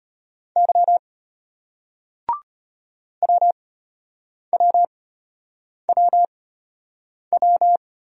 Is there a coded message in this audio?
Yes: Morse code from 0.6 to 7.8 s, heard as a keyed tone, about 33 wpm, spaced as for 8 wpm, 700 Hz. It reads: Y WWWW